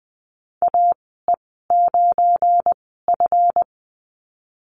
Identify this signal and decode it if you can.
Morse code, heard as a keyed tone, about 20 wpm, spaced as for 15 wpm, 700 Hz. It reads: AE9F
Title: 179 Lesson 29g - Slash - Call Signs 20wpm